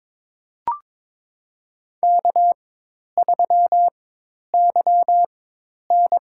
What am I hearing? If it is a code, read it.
Morse code, heard as a keyed tone, about 22 wpm, spaced as for 10 wpm, 700 Hz. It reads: K3YN
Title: USA Call Signs - 4 Characters - 4x Character Spacing - 22wpm